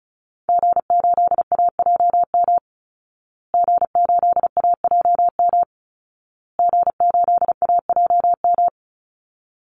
Morse code, heard as a keyed tone, about 34 wpm, 700 Hz. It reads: G8AJM G8AJM G8AJM
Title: Call Signs Repeated 2x - 35wpm